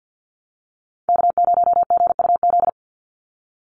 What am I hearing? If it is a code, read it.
Morse code, heard as a keyed tone, about 50 wpm, 700 Hz. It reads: K0GUZ
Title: Call Signs - 50wpm